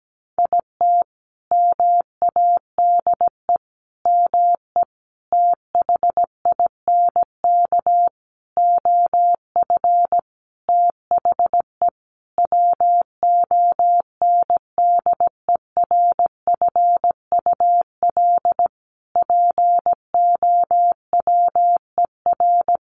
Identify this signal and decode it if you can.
Morse code, heard as a keyed tone, about 17 wpm, 700 Hz. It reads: IT MADE ME THINK OF THE WONDERFUL POWER
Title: Sentences from Top 1100 Words 17wpm